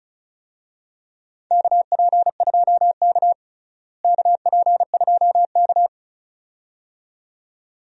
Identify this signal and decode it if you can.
Morse code, heard as a keyed tone, about 36 wpm, 700 Hz. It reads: KP2K KP2K
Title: DXCC-100 Call Signs - 4 Characters - Repeated 2x - 35wpm